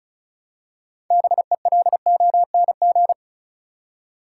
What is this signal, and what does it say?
Morse code, heard as a keyed tone, about 36 wpm, 700 Hz. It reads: BELONG
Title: Six Letter Words - 35wpm